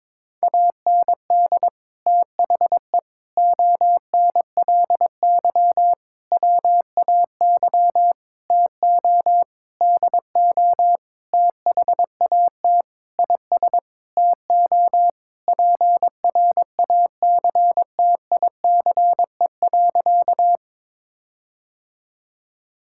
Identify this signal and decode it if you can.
Morse code, heard as a keyed tone, about 22 wpm, 700 Hz. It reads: AND THE ONLY WAY TO DO THAT IS TO PRACTICE.
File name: Sentences with 9-Word Phrases - 22wpm